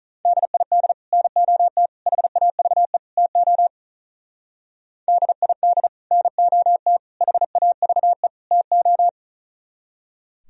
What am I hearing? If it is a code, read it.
Morse code, heard as a keyed tone, about 42 wpm, 700 Hz. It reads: DID NOT HAVE TO DID NOT HAVE TO
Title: Speed-Racing - Sets of 4 Words 35wpm